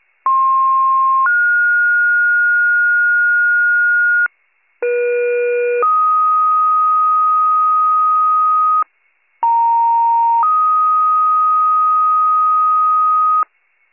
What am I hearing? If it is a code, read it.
Two-tone page, heard as a one-second tone, then a three-second tone, about 1060 then 1430 Hz, about 480 then 1250 Hz, about 930 then 1250 Hz. It is heard from 0.3 to 13.4 s.